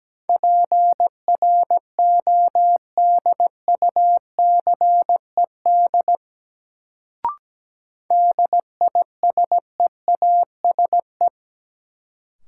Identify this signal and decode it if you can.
Morse code, heard as a keyed tone, about 17 wpm, 700 Hz. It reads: PRODUCED DISEASE
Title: Sentences from Top 1000 Words - Review Part 6 - New Word Drill - 17wpm